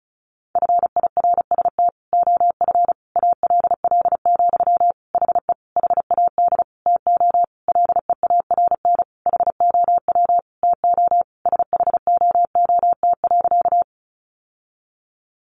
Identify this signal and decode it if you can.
Morse code, heard as a keyed tone, about 34 wpm, 700 Hz. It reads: FIRST OF ALL, HE HAD TO LEARN HOW TO SHOOT.
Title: Sentences from Top 1200 Words 35wpm